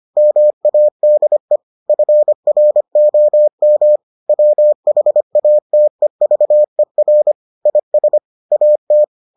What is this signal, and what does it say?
Morse code, heard as a keyed tone, about 25 wpm, 600 Hz. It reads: MADE FROM WHATEVER IS AT